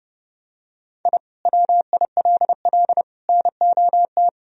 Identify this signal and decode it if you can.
Morse code, heard as a keyed tone, about 30 wpm, 700 Hz. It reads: I WILL NOT
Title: Sentences from Top 1000 Words - Review Part 2 - 30wpm